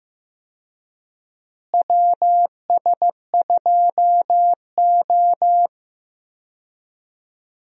Morse code, heard as a keyed tone, about 15 wpm, 700 Hz. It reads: WS2O
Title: USA Call Signs - 4 Characters - Repeated 3x - 15wpm